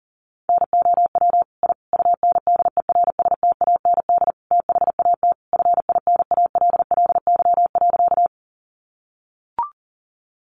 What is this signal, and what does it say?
Morse code, heard as a keyed tone, about 40 wpm, 700 Hz. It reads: NOW I UNDERSTAND THAT FINALLY.
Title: Sentences from Top 1000 Words 40wpm